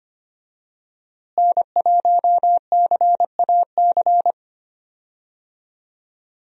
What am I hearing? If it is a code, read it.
Morse code, heard as a keyed tone, about 25 wpm, 700 Hz. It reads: N1CAC
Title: USA Call Signs - 5 Characters - Repeated 2x - 25wpm